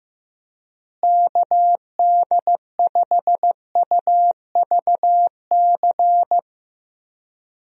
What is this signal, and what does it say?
Morse code, heard as a keyed tone, about 16 wpm, 700 Hz. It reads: KD5UVC
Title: USA Call Signs - 6 Characters - Repeated 3x - 15wpm